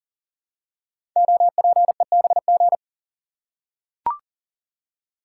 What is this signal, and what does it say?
Morse code, heard as a keyed tone, about 40 wpm, 700 Hz. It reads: OPEBG
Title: Code Groups - 5 Letter:Numbers - 40wpm